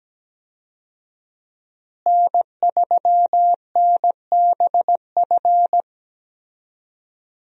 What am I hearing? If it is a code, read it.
Morse code, heard as a keyed tone, about 17 wpm, 700 Hz. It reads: N3NBF
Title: USA Call Signs - 5 Characters - Repeated 2x - 17wpm